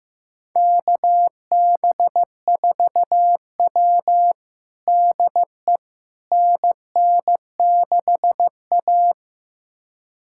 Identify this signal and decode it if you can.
Morse code, heard as a keyed tone, about 15 wpm, 700 Hz. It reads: KB4W DE NN6A
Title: USA Callsign DE Callsign - 4 Characters - 15wpm